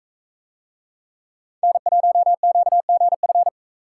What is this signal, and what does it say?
Morse code, heard as a keyed tone, about 40 wpm, 700 Hz. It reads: N1QGF